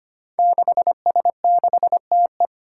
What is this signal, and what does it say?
Morse code, heard as a keyed tone, about 25 wpm, 700 Hz. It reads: 6S6TE
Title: Five Challenging Letter-Number Combinations 25wpm